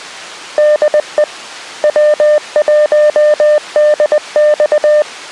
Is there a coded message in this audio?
Yes: Morse code from 0.6 to 5.0 s, heard as a keyed tone, about 20 wpm, 600 Hz. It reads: DE W1DX